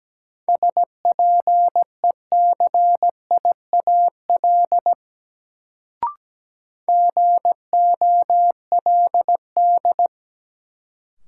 Morse code, heard as a keyed tone, about 17 wpm, 700 Hz. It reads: SPECIAL GOLD
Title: Sentences from Top 1000 Words - Review Part 3 - New Word Drill - 17wpm